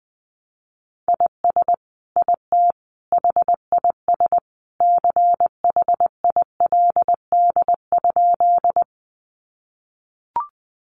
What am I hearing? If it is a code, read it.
Morse code, heard as a keyed tone, about 20 wpm, 700 Hz. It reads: IS IT HIS CHILD?
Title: Sentences from Top 800 Words 20wpm